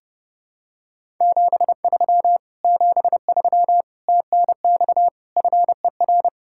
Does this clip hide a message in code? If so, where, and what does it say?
Yes: Morse code from 1.2 to 6.3 s, heard as a keyed tone, about 29 wpm, 700 Hz. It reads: 73 73 TNX FER